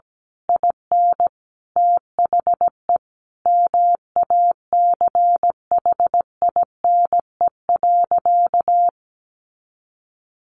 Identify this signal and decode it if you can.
Morse code, heard as a keyed tone, about 17 wpm, 700 Hz. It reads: IN THE MACHINE.
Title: A Sound Of Thunder 17wpm